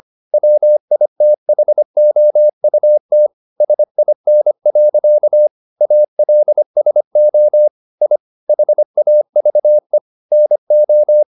Morse code, heard as a keyed tone, about 25 wpm, 600 Hz. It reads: WITHOUT SIN. ALSO I HAVE NO